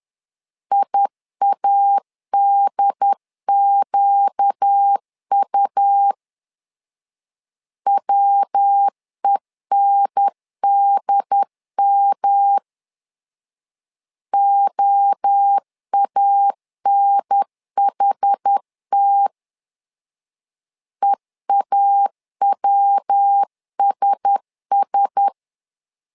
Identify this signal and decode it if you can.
Morse code, heard as a keyed tone, about 10 wpm, 800 Hz. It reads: IADQU WENDM OANHT EAWSS
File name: LEZIONI / 4 Q1S 10wpm 2133